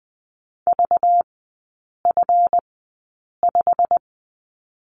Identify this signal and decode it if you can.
Morse code, heard as a keyed tone, about 20 wpm, spaced as for 8 wpm, 700 Hz. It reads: VF5
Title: 191 Lesson 31e - Letter v - Three Characters 20wpm